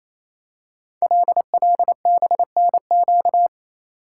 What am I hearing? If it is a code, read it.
Morse code, heard as a keyed tone, about 28 wpm, 700 Hz. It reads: LLBNQ
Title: Code Groups - 5 Letters - 28wpm